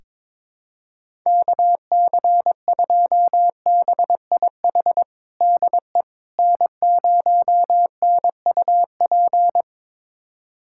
Morse code, heard as a keyed tone, about 22 wpm, 700 Hz. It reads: KC2BIH DE N0NUP
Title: USA Callsign DE Callsign 22wpm